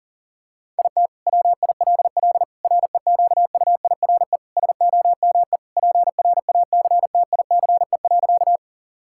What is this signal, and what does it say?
Morse code, heard as a keyed tone, about 40 wpm, 700 Hz. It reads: IT WILL REQUIRE SOME PRACTICE.
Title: Sentences from Top 1000 Words - Review Part 6 - 40wpm